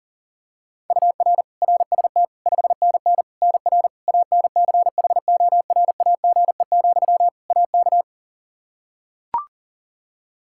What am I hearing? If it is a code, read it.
Morse code, heard as a keyed tone, about 40 wpm, 700 Hz. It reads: UR RST 5NN NR ANCHORAGE, AK
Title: Practice QSO - Exchange 1 - Beginning - 40wpm